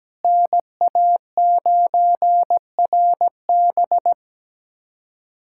Morse code, heard as a keyed tone, about 17 wpm, 700 Hz. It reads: NA9RB
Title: USA Callsign DE Callsign - 5 Characters - 17wpm